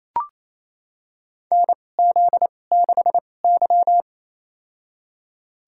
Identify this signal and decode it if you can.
Morse code, heard as a keyed tone, about 27 wpm, spaced as for 20 wpm, 700 Hz. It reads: NZ6Y
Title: USA Call Signs - 4 Characters - 2x Character Spacing - 28wpm